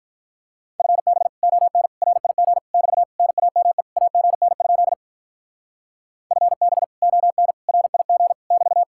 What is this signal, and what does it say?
Morse code, heard as a keyed tone, about 43 wpm, 700 Hz. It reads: FB ON RIG <BT> NAME AGN? FB ON RIG <BT>
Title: Speed-Racing Common QSO Phrases - 35wpm